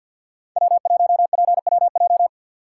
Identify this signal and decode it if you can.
Morse code, heard as a keyed tone, about 52 wpm, 700 Hz. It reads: W1PWJ
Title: USA Call Signs - 5 Characters - 4x Character Spacing - 50wpm